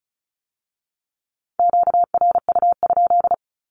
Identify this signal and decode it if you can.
Morse code, heard as a keyed tone, about 36 wpm, 700 Hz. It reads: QRU?
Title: Q-Codes with Spoken Meaning - 35wpm